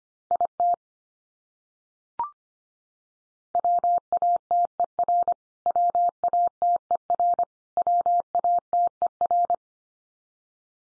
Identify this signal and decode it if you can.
Morse code, heard as a keyed tone, about 25 wpm, 700 Hz. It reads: IT WATER WATER WATER
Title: Top 100 Words 25wpm x3